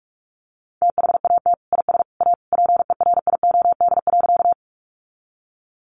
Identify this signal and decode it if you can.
Morse code, heard as a keyed tone, about 44 wpm, 700 Hz. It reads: THAT IS A PERIOD.
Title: Sentences from Top 800 Words 45wpm